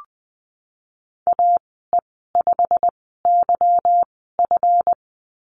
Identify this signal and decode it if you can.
Morse code, heard as a keyed tone, about 20 wpm, spaced as for 15 wpm, 700 Hz. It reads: AE5YF
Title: 173 Lesson 28g - Number 9 - Call Signs 20wpm